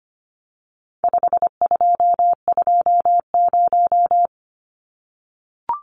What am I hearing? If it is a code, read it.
Morse code, heard as a keyed tone, about 25 wpm, 700 Hz. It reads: 5220
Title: Numbers - 4-Digit 25wpm